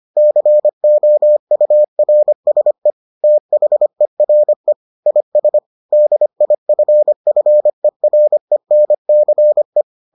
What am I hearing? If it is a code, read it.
Morse code, heard as a keyed tone, about 25 wpm, 600 Hz. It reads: COURSE THERE IS DIFFERENCE